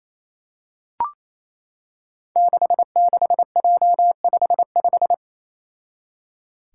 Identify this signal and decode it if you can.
Morse code, heard as a keyed tone, about 28 wpm, 700 Hz. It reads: 66J55
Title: Five Challenging Letter-Number Combinations 28wpm